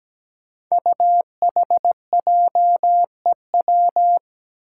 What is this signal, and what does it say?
Morse code, heard as a keyed tone, about 17 wpm, 700 Hz. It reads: UHJEW
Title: Five Challenging Letter-Number Combinations 17wpm